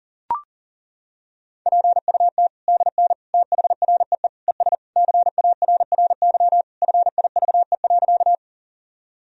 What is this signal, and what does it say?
Morse code, heard as a keyed tone, about 40 wpm, 700 Hz. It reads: PUT DN THREE ES CARRY FIVE.
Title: Sentences from Top 400 Words and Abbreviations 40wpm